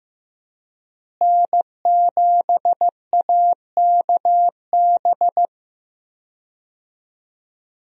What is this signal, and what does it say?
Morse code, heard as a keyed tone, about 15 wpm, 700 Hz. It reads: N7AKB
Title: USA Call Signs - 5 Characters - Repeated 2x - 15wpm